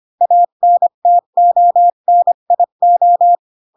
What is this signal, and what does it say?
Morse code, heard as a keyed tone, about 25 wpm, 700 Hz. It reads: ANTONIO
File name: Pinocchio / Pinocchio ch2 20 25 ik1hge